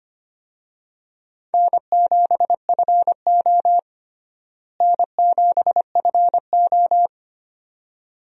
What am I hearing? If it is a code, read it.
Morse code, heard as a keyed tone, about 25 wpm, 700 Hz. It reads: N7FO N7FO
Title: USA Call Signs - 4 Characters - Repeated 2x - 25wpm